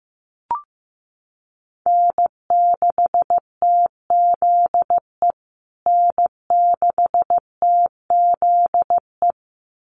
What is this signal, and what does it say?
Morse code, heard as a keyed tone, about 15 wpm, 700 Hz. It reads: N6TZE N6TZE
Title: USA Call Signs Repeated 2x 15wpm